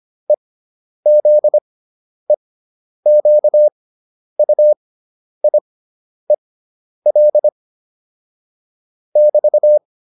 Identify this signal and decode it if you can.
Morse code, heard as a keyed tone, about 25 wpm, spaced as for 10 wpm, 600 Hz. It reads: EZEQUIEL <BT>